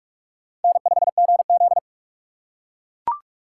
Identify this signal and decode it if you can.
Morse code, heard as a keyed tone, about 44 wpm, 700 Hz. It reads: N5GZ